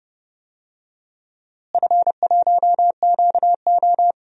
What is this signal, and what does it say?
Morse code, heard as a keyed tone, about 30 wpm, 700 Hz. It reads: F1QO
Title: DXCC-100 Call Signs - 4 Characters - Repeated 2x - 30wpm